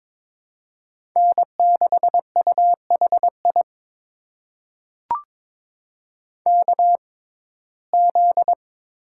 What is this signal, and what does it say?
Morse code, heard as a keyed tone, about 22 wpm, spaced as for 16 wpm, 700 Hz. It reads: N6UHI KZ
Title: USA Call Signs - 6x Character Spacing - 22wpm